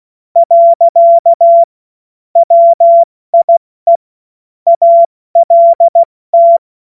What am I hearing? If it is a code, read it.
Morse code, heard as a keyed tone, about 16 wpm, 680 Hz. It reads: . WIE ALT
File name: Morsekurs / Witze / Witz 13. September 2022 70 F80